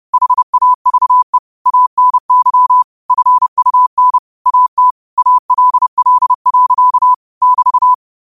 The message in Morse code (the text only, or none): HAVE ANY FUN AT ALL. <BT>